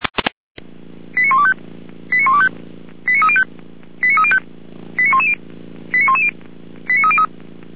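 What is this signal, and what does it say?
Five-tone selective call (ZVEI), heard as quick runs of five tones, tones of about 80 ms. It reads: 89125 89125 89305 89305 89110 89110 89303
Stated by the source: ZVEI five-tone call 89276 received on SDR, fire brigade feuerwehr-orlamuende.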